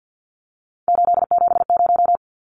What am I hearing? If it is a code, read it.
Morse code, heard as a keyed tone, about 50 wpm, 700 Hz. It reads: 870
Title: Numbers - 3-Digit 50wpm